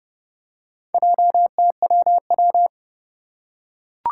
Morse code, heard as a keyed tone, about 30 wpm, 690 Hz. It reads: JTWW
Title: Four Challenging Letter-Number Combinations 30wpm